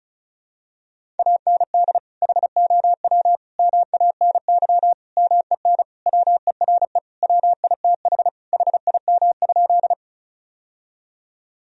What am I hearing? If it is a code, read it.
Morse code, heard as a keyed tone, about 34 wpm, 700 Hz. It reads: AND HOW MANY MEN WERE WITH HIM?
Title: Sentences from Top 1000 Words - Review Part 1 - 35wpm